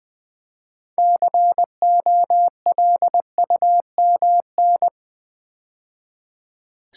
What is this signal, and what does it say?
Morse code, heard as a keyed tone, about 20 wpm, 700 Hz. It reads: COLUMN
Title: Build A Word - 3 Letters At A Time - 6-Letter Words - 20wpm